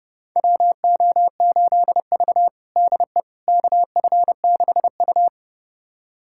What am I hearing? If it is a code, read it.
Morse code, heard as a keyed tone, about 30 wpm, 700 Hz. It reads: WO8V DE KF6U
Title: USA Callsign DE Callsign - 4 Characters - 30wpm